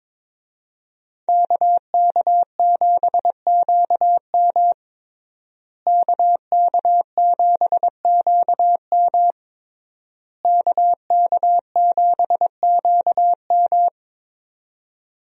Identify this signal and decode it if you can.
Morse code, heard as a keyed tone, about 21 wpm, 700 Hz. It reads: KK7QM KK7QM KK7QM
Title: USA Call Signs - 5 Characters - Repeated 3x - 22wpm